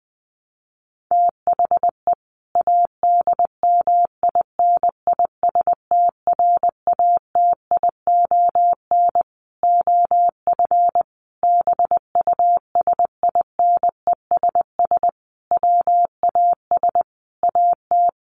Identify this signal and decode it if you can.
Morse code, heard as a keyed tone, about 19 wpm, 700 Hz. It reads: THE ADMINISTRATION OF BUSINESS WAS AT
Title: Sentences from Top 1300 Words 20wpm